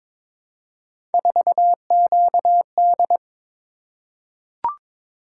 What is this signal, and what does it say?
Morse code, heard as a keyed tone, about 22 wpm, 700 Hz. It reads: 4QD
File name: practice / 22wpm / Three Letter-Number Combinations - 22wpm